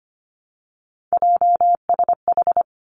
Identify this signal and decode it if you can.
Morse code, heard as a keyed tone, about 25 wpm, 700 Hz. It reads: JSH
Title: Three Challenging Letter-Number Combinations 25wpm